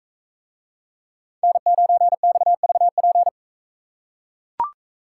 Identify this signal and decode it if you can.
Morse code, heard as a keyed tone, about 41 wpm, 700 Hz. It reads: N9XVP